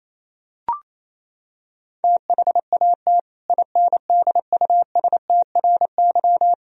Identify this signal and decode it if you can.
Morse code, heard as a keyed tone, about 28 wpm, 700 Hz. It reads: THAT INDUSTRY
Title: Sentences from Top 1000 Words - Review Part 4 - 28wpm